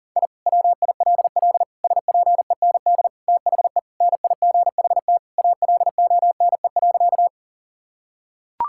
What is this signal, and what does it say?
Morse code, heard as a keyed tone, about 40 wpm, 700 Hz. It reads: I WILL SPEND THE NIGHT ALONE.
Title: Sentences from Top 1000 Words - Review Part 5 - 40wpm